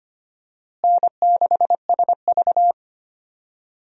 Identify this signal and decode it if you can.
Morse code, heard as a keyed tone, about 25 wpm, 700 Hz. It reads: N6SV
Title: USA Call Signs - 4 Characters - Repeated 3x - 25wpm